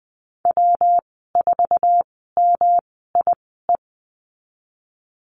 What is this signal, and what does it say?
Morse code, heard as a keyed tone, about 20 wpm, spaced as for 15 wpm, 700 Hz. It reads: W4MIE
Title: 122 Lesson 20g - Number 3 - Call Signs 20wpm